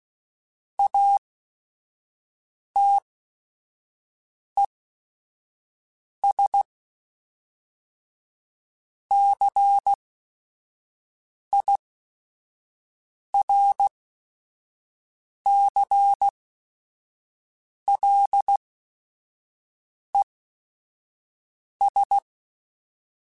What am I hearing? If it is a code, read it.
Morse code, heard as a keyed tone, about 16 wpm, spaced as for 5 wpm, 800 Hz. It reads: ATES CIRCLES
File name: Technical Subjects / Technical Subjects-mp3s / XL @5